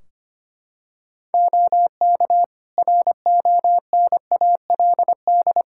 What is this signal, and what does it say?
Morse code, heard as a keyed tone, about 25 wpm, 700 Hz. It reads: OK RONALD
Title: Practice QSO - Exchange 4 - Ending - 25wpm